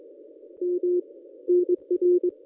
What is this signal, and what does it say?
Morse code, heard as a keyed tone, about 23 wpm, 410 Hz. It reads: M NR
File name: contests / ss-cw-k5tr-op-n5rz -1000